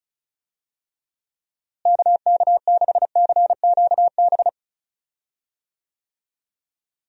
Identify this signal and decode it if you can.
Morse code, heard as a keyed tone, about 34 wpm, 700 Hz. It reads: KK6CQB
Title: USA Call Signs - 6 Characters - Repeated 2x - 35wpm